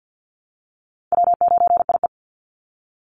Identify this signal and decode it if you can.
Morse code, heard as a keyed tone, about 48 wpm, 700 Hz. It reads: W9IE
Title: Easy Call Signs Repeated 3x - 50wpm